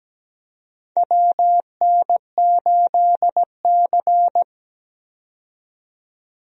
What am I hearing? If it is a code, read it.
Morse code, heard as a keyed tone, about 17 wpm, 700 Hz. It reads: WN8C
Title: USA Call Signs - 4 Characters - 17wpm